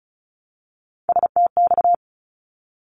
Morse code, heard as a keyed tone, about 34 wpm, 700 Hz. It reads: STX
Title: ARRL Sections 35wpm